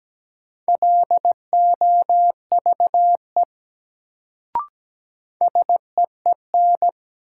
Morse code, heard as a keyed tone, about 17 wpm, 700 Hz. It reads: LOVE SEEN
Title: Sets of 5 Words Encore - New Word Drill - 17wpm